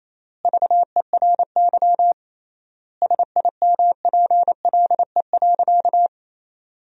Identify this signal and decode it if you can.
Morse code, heard as a keyed tone, about 27 wpm, 700 Hz. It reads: VERY SIMPLE.
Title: 3x Word-Spacing - Sentences from Top 500 Words - 28wpm